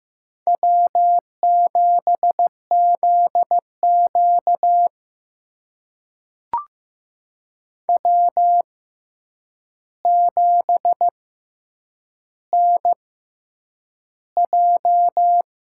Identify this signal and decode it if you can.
Morse code, heard as a keyed tone, about 15 wpm, spaced as for 8 wpm, 700 Hz. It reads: W7ZQ W7NJ
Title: USA Call Signs - 6x Character Spacing - 15wpm